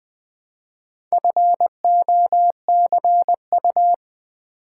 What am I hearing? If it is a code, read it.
Morse code, heard as a keyed tone, about 20 wpm, 700 Hz. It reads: FOCU
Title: Build A Word - 1 Letter At A Time - 5-Letter Words - 20wpm